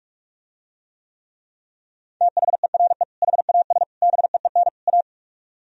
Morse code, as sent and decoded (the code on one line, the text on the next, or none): - .... .-. / .... .- ... / -... . -. / .-
THERE HAS BEEN A